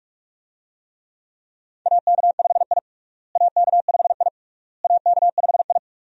Morse code, heard as a keyed tone, about 42 wpm, 700 Hz. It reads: AK5I AK5I AK5I
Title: USA Call Signs - 4 Characters - Repeated 3x - 45wpm